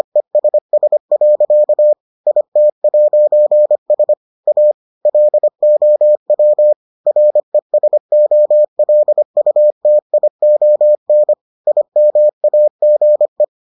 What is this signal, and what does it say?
Morse code, heard as a keyed tone, about 25 wpm, 600 Hz. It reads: ESS. IT'S A LOW RESOLUTION IMAGE